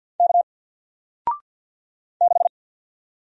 Morse code, heard as a keyed tone, about 48 wpm, 700 Hz. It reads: K 6
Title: Single Letter-Number Alt 50wpm